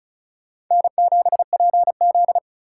Morse code, heard as a keyed tone, about 44 wpm, 700 Hz. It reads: N7PZ